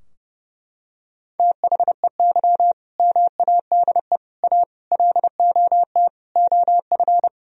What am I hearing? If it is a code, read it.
Morse code, heard as a keyed tone, about 30 wpm, 700 Hz. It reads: THEY MADE A LOT OF
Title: Sentences from Top 1000 Words - Review Part 4 - 30wpm